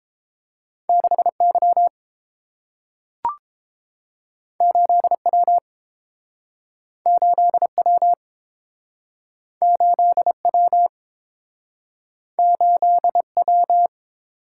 Morse code, heard as a keyed tone, about 30 wpm, 700 Hz. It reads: 6Y 8W 8W 8W 8W